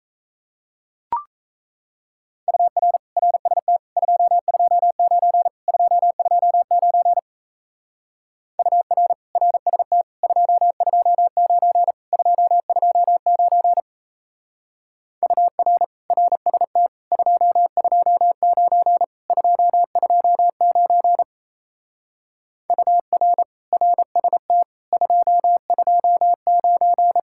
Morse code, heard as a keyed tone, about 37 wpm, 700 Hz. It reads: UR RST 229 229 UR RST 229 229 UR RST 229 229 UR RST 229